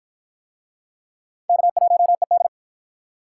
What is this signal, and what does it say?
Morse code, heard as a keyed tone, about 48 wpm, 700 Hz. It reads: K1ED